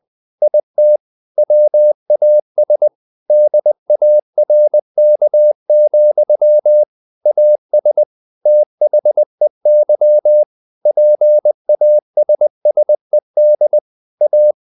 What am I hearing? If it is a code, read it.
Morse code, heard as a keyed tone, about 20 wpm, 600 Hz. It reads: IT WAS DARK, AS THEY PASSED A
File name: the old man and the sea part 2 20wpm